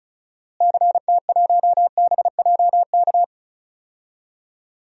Morse code, heard as a keyed tone, about 36 wpm, 700 Hz. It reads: CT1BJK
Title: DXCC-100 Call Signs - 6 Characters - Repeated 2x - 35wpm